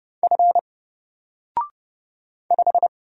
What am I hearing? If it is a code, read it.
Morse code, heard as a keyed tone, about 30 wpm, 700 Hz. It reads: F 5
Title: Single Character (Letter:Number:Punctuation:Prosign) 30wpm